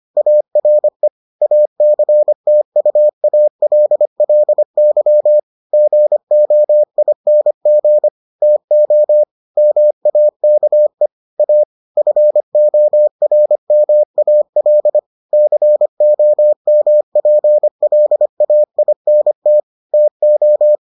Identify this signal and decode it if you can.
Morse code, heard as a keyed tone, about 25 wpm, 600 Hz. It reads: ARE ACTUALLY GOING TO MAKE A FORMAL COMPLAINT TO